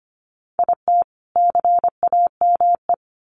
Morse code, heard as a keyed tone, about 25 wpm, 700 Hz. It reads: IT CAME
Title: Sentences from Top 1500 Words 25wpm